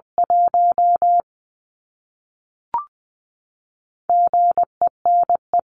Morse code, heard as a keyed tone, about 20 wpm, 700 Hz. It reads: 1 GENE
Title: CWT Exchange 20wpm